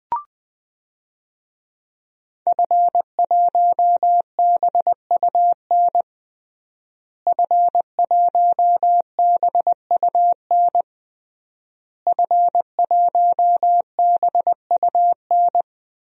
Morse code, heard as a keyed tone, about 21 wpm, 700 Hz. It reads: F1BUN F1BUN F1BUN
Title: DXCC-100 Call Signs - 5 Characters - Repeated 3x - 20wpm